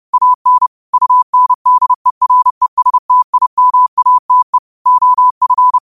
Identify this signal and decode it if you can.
Morse code, heard as a keyed tone, about 30 wpm, 1000 Hz. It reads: AN UNDERESTIMATE OF
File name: qotdfiles / 30wpm